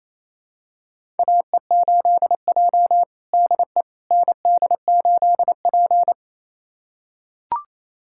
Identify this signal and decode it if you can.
Morse code, heard as a keyed tone, about 28 wpm, 700 Hz. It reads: AE8J DE ND8P
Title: USA Callsign DE Callsign - 4 Characters - 28wpm